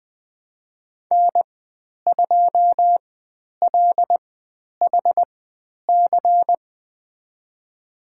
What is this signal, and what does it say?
Morse code, heard as a keyed tone, about 20 wpm, spaced as for 10 wpm, 700 Hz. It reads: N2LHC
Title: 095 Lesson 16g - Letter c - Call Signs 20wpm